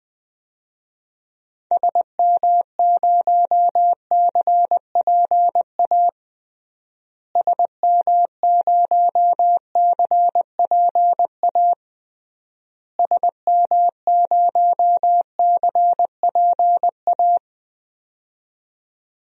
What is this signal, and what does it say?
Morse code, heard as a keyed tone, about 20 wpm, 700 Hz. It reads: SM0CPA SM0CPA SM0CPA
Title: DXCC-100 Call Signs - 6 Characters - Repeated 3x - 20wpm